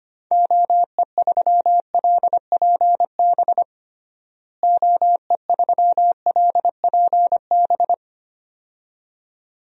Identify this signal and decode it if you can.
Morse code, heard as a keyed tone, about 25 wpm, 700 Hz. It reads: OE3LPB OE3LPB
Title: DXCC-100 Call Signs - 6 Characters - Repeated 2x - 25wpm